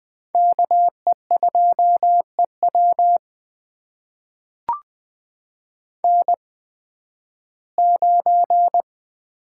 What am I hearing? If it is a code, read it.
Morse code, heard as a keyed tone, about 20 wpm, spaced as for 13 wpm, 700 Hz. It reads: KE2EW N9
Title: USA Call Signs - 5 Characters - 8x Character Spacing - 20wpm